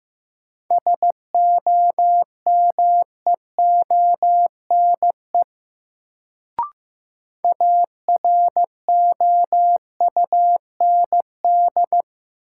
Morse code, heard as a keyed tone, about 16 wpm, 700 Hz. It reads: SOMEONE AROUND
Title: Sentences from Top 600 Words - New Word Drill - 15wpm